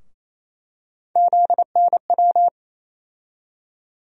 Morse code, heard as a keyed tone, about 27 wpm, 700 Hz. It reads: ZNW
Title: USA Call Sign Postfixes - 3 Letters 28wpm